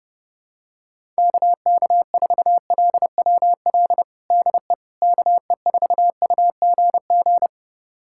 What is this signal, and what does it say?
Morse code, heard as a keyed tone, about 32 wpm, 700 Hz. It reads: KK4LWL DE KE4UGG
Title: USA Callsign DE Callsign - 6 Characters - 30wpm